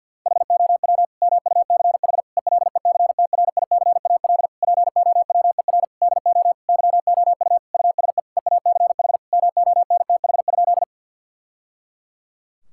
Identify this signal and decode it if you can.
Morse code, heard as a keyed tone, about 52 wpm, 700 Hz. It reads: HOW MUCH ELECTRICAL POWER DO YOU USE EACH MONTH?
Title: Sentences from Top 1000 Words - Review Part 6 - 50wpm